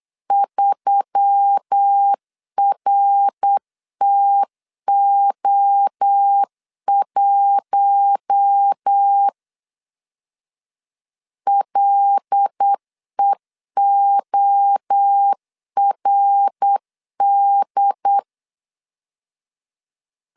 Morse code, heard as a keyed tone, about 8 wpm, 800 Hz. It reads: 3RTO1 LEORD
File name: CW / DET / 1 6 1500C S B 8wpm 5142